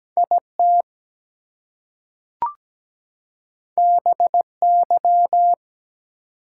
Morse code, heard as a keyed tone, about 17 wpm, 700 Hz. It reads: IT BY
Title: Two Letter Words - 17wpm